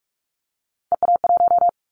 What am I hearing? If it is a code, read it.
Morse code, heard as a keyed tone, about 46 wpm, 700 Hz. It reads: EA1
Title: Callsign Prefixes for Top 25 DXCC - 45wpm